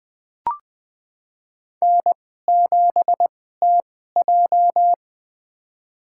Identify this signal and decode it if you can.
Morse code, heard as a keyed tone, about 20 wpm, spaced as for 15 wpm, 700 Hz. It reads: N7TJ
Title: USA Call Signs - 4 Characters - 2x Character Spacing - 20wpm